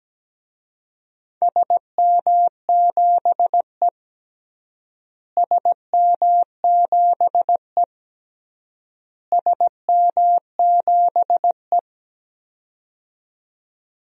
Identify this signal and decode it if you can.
Morse code, heard as a keyed tone, about 17 wpm, 700 Hz. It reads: SM7E SM7E SM7E
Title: DXCC-100 Call Signs - 4 Characters - Repeated 3x - 17wpm